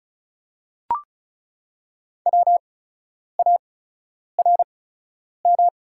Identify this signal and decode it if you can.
Morse code, heard as a keyed tone, about 36 wpm, spaced as for 10 wpm, 690 Hz. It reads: WARM
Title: 8x Character Spacing - Top 500 Words - 35wpm